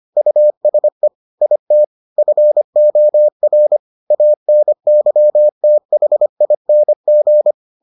Morse code, heard as a keyed tone, about 25 wpm, 600 Hz. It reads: USE IT FOR ANYTHING